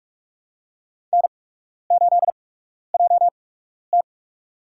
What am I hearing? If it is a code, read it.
Morse code, heard as a keyed tone, about 46 wpm, spaced as for 12 wpm, 690 Hz. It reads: N8JT
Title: USA Call Signs - 4 Characters - 8x Character Spacing - 45wpm